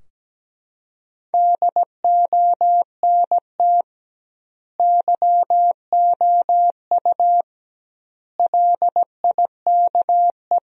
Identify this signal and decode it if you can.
Morse code, heard as a keyed tone, about 17 wpm, 700 Hz. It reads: DONT YOU LIKE
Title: 2x Word-Spacing - Sentences from Top 300 Words - 17wpm